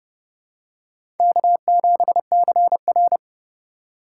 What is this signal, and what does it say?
Morse code, heard as a keyed tone, about 30 wpm, 700 Hz. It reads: K7CR
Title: USA Call Signs - 4 Characters - 4x Character Spacing - 30wpm